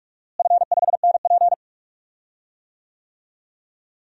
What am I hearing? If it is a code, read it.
Morse code, heard as a keyed tone, about 47 wpm, 690 Hz. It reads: F5NP
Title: DXCC-100 Call Signs - 4 Characters - Repeated 3x - 45wpm